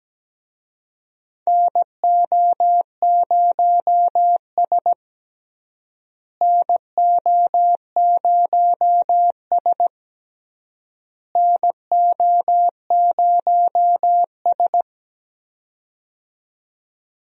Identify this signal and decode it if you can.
Morse code, heard as a keyed tone, about 17 wpm, 700 Hz. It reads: NO0S NO0S NO0S
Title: USA Call Signs - 4 Characters - Repeated 3x - 17wpm